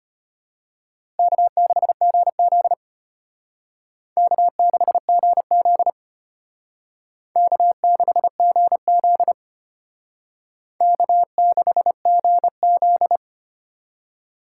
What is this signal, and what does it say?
Morse code, heard as a keyed tone, about 33 wpm, 700 Hz. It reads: K6GZ K6GZ K6GZ K6GZ